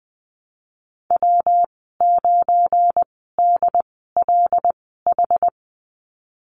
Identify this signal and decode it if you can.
Morse code, heard as a keyed tone, about 19 wpm, spaced as for 14 wpm, 700 Hz. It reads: W9DLH
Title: 173 Lesson 28g - Number 9 - Call Signs 20wpm